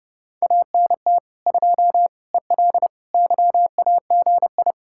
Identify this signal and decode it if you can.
Morse code, heard as a keyed tone, about 37 wpm, 700 Hz. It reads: ANT 2 EL YAGI
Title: Speed-Racing Common QSO Phrases Encore - 30wpm